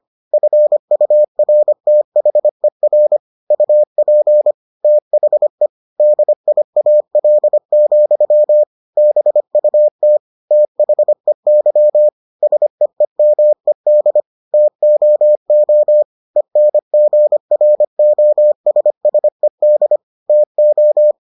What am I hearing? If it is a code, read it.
Morse code, heard as a keyed tone, about 25 wpm, 600 Hz. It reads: FURTHER UP THE DIAL, BUT THEY SEEMED TOO ENGROSSED TO